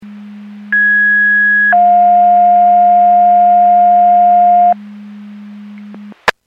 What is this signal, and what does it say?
Two-tone page, heard as a one-second tone, then a three-second tone, about 1650 then 710 Hz. It is heard from 0.7 to 4.7 s.
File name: Portage County 154.3850 Fire EMS Paging